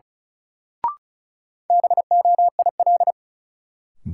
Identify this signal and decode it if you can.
Morse code, heard as a keyed tone, about 36 wpm, 700 Hz. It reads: BOIL
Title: Sets of 5 Words Encore - New Word Drill - 35wpm